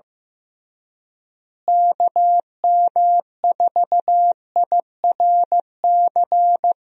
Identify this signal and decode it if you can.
Morse code, heard as a keyed tone, about 15 wpm, 700 Hz. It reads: KM4IRC
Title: USA Call Signs - 6 Characters - Repeated 3x - 15wpm